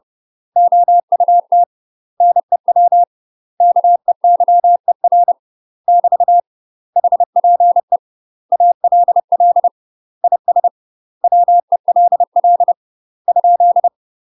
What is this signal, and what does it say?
Morse code, heard as a keyed tone, about 30 wpm, 700 Hz. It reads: OUT NEW KEYER <BT> HPE ALL IS WELL ?
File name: UK MASTER QSOs TRAINING / 30WPM / UK MASTER QSOs 0004